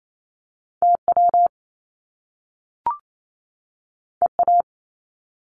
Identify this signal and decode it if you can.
Morse code, heard as a keyed tone, about 28 wpm, 700 Hz. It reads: TW EA
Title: Top Two Letter Combinations in Words 28wpm